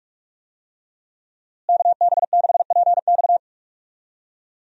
Morse code, heard as a keyed tone, about 46 wpm, 700 Hz. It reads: KB6PX
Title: USA Call Signs - 5 Characters - Repeated 3x - 45wpm